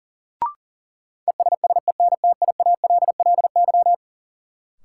Morse code, heard as a keyed tone, about 40 wpm, 700 Hz. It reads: ESSENTIALLY
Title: Sentences from Top 1300 Words - New Word Drill - 40wpm